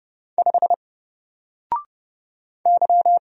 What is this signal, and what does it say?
Morse code, heard as a keyed tone, about 29 wpm, 700 Hz. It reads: 5 Y
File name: Single-Letter-Number-30wpm